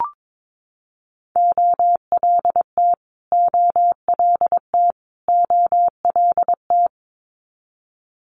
Morse code, heard as a keyed tone, about 22 wpm, 700 Hz. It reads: OLT OLT OLT
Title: Multi-letter Phonemes Repeated 3x 22wpm